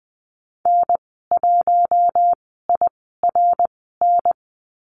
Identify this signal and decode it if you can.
Morse code, heard as a keyed tone, about 20 wpm, spaced as for 14 wpm, 700 Hz. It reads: N1IRN
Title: 062 Lesson 11g - Letter h - Call Signs 20wpm